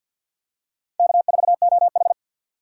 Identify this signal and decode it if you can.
Morse code, heard as a keyed tone, about 47 wpm, 700 Hz. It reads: K4OH
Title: USA Call Signs - 4 Characters - 6x Character Spacing - 50wpm